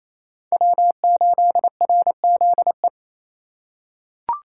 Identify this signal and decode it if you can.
Morse code, heard as a keyed tone, about 28 wpm, 700 Hz. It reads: W8RZE
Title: USA Call Signs - 5 Characters - 2x Character Spacing - 28wpm